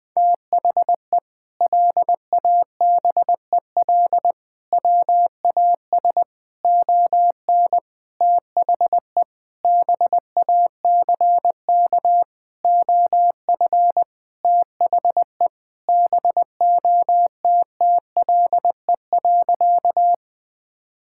Morse code, heard as a keyed tone, about 20 wpm, 700 Hz. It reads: THE LABEL WAS ON THE BACK OF THE BOTTLE.
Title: Sentences with 4-Word Phrases - Spoken Beginning - 20wpm